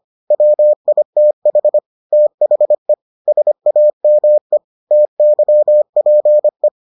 Morse code, heard as a keyed tone, about 25 wpm, 600 Hz. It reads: WITH THE SAME TYPE